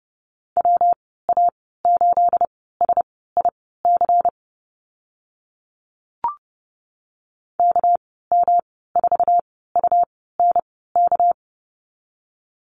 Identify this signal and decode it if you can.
Morse code, heard as a keyed tone, about 30 wpm, spaced as for 17 wpm, 700 Hz. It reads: WA8SIC KM4UNK
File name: Lesson 37g - Letter x - Call Signs 30wpm